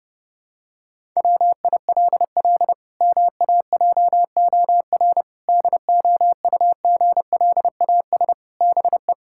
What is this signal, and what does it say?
Morse code, heard as a keyed tone, about 30 wpm, 700 Hz. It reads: WILL MAJOR DOUGLAS BE
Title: Pangram Challenge v2 - 30wpm